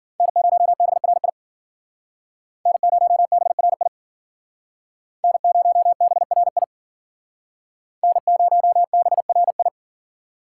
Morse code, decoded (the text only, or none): N0BRI N0BRI N0BRI N0BRI